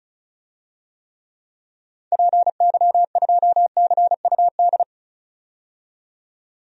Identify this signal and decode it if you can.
Morse code, heard as a keyed tone, about 36 wpm, 700 Hz. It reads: PY2CUD
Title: DXCC-100 Call Signs - 6 Characters - 35wpm